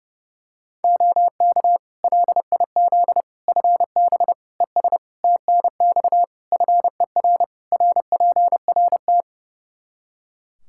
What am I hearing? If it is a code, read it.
Morse code, heard as a keyed tone, about 30 wpm, 700 Hz. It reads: OK LIZ FB ES TNX FER RPRT
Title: K1USN SST QSO Practice 30wpm